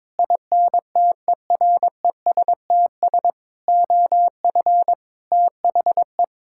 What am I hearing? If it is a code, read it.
Morse code, heard as a keyed tone, about 22 wpm, 700 Hz. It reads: INTERESTS OF THE
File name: practice / 22wpm / Sets of 6 Words - 22wpm